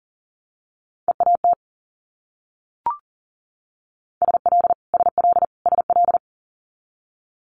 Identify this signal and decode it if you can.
Morse code, heard as a keyed tone, about 38 wpm, 700 Hz. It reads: EAT SL SL SL
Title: Multi-letter Phonemes Repeated 3x 40wpm